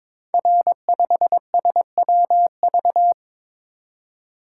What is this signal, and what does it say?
Morse code, heard as a keyed tone, about 22 wpm, 700 Hz. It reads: R5SWV
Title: Code Groups - 5 Letter:Numbers - 22wpm